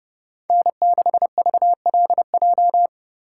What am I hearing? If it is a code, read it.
Morse code, heard as a keyed tone, about 31 wpm, 700 Hz. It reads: N6VLJ